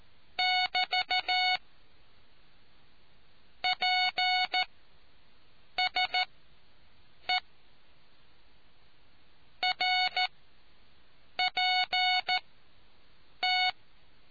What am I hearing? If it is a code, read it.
Morse code, heard as a keyed tone, about 14 wpm, spaced as for 6 wpm, 740 Hz. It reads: <BT> PSE RPT